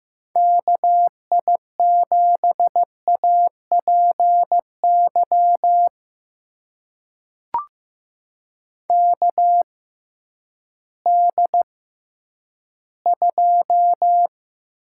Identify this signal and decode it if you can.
Morse code, heard as a keyed tone, about 15 wpm, spaced as for 10 wpm, 700 Hz. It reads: KI7APY KD2
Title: USA Call Signs - 6 Characters - 6x Character Spacing - 15wpm